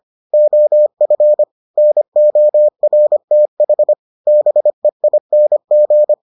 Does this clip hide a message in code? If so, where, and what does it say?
Yes: Morse code from 0.3 to 6.1 s, heard as a keyed tone, about 25 wpm, 600 Hz. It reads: OF NORTH BEING